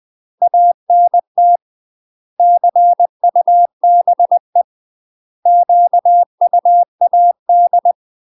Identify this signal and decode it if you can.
Morse code, heard as a keyed tone, about 20 wpm, 700 Hz. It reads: ANT CUBE QUAD
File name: UK MASTER QSOs TRAINING/20WPM